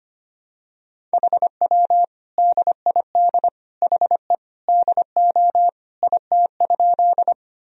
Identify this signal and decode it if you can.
Morse code, heard as a keyed tone, about 25 wpm, 700 Hz. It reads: HW DID HE DO IT?
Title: Sentences from Top 100 Words and Abbreviations 25wpm